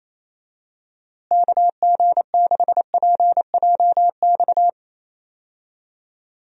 Morse code, heard as a keyed tone, about 28 wpm, 700 Hz. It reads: KG6PJX
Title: USA Call Signs - 6 Characters - Repeated 2x - 28wpm